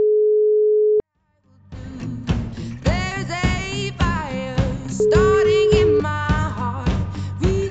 Ringback tone (Europe, 425 Hz), with noise.